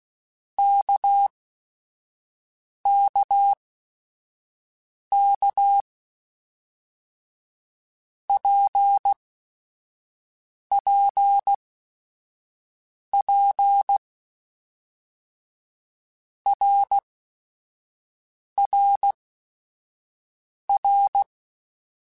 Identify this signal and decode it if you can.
Morse code, heard as a keyed tone, about 16 wpm, spaced as for 5 wpm, 800 Hz. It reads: KKK PPP RRR
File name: Basic Sets / Basic Letter Groups Collective7 / RKPX@5